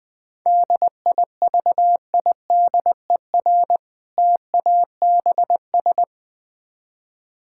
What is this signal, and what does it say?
Morse code, heard as a keyed tone, about 21 wpm, 700 Hz. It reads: DIVIDER TABS
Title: Difficult Words - Speed-Racing with Hints - 20wpm